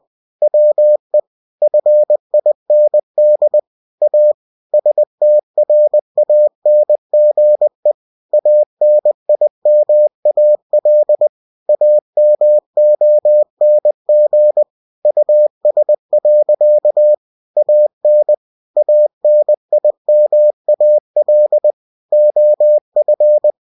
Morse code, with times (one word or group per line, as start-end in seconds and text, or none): 0.4-1.2 WE
1.6-3.6 FIND
4.0-4.3 A
4.7-7.9 STRANGE
8.3-11.3 ANIMAL
11.7-14.6 AMONG
15.1-17.1 US.
17.6-18.3 AN
18.8-21.7 ANIMAL
22.1-23.5 OF